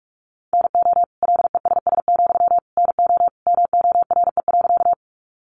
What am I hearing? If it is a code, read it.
Morse code, heard as a keyed tone, about 44 wpm, 700 Hz. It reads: NO LESS, NO MORE.
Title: Sentences from Top 400 Words 45wpm